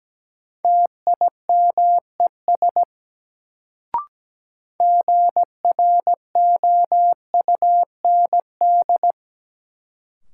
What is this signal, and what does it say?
Morse code, heard as a keyed tone, about 17 wpm, 700 Hz. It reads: TIMES GROUND
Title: Sets of 3 Words - New Word Drill - 17wpm